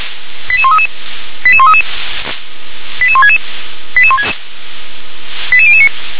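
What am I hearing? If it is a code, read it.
Five-tone selective call (ZVEI), heard as quick runs of five tones, tones of about 70 ms. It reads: 88133 88133 88166 88009
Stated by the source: ZVEI five-tone selective call 88058, received by SDR.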